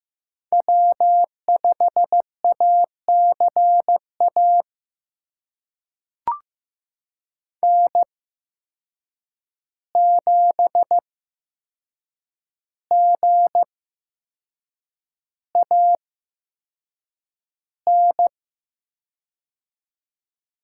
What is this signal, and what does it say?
Morse code, heard as a keyed tone, about 15 wpm, spaced as for 6 wpm, 700 Hz. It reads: W5ACA N7GAN
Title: USA Call Signs - 5 Characters - 8x Character Spacing - 15wpm